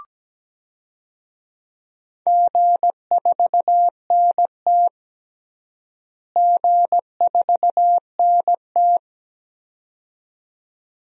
Morse code, heard as a keyed tone, about 17 wpm, 700 Hz. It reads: G4NT G4NT
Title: DXCC-100 Call Signs - 4 Characters - Repeated 2x - 17wpm